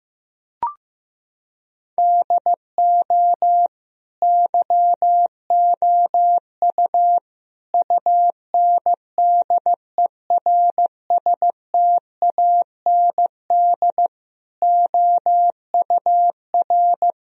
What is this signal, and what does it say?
Morse code, heard as a keyed tone, about 15 wpm, 700 Hz. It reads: DO YOU UNDERSTAND OUR
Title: Sentences from Top 1000 Words - Review Part 3 - 15wpm